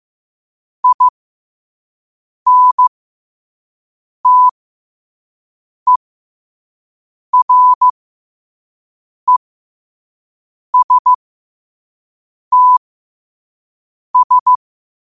Morse code, heard as a keyed tone, about 15 wpm, spaced as for 5 wpm, 1000 Hz. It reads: INTERESTS
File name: qotdfiles / 5wpm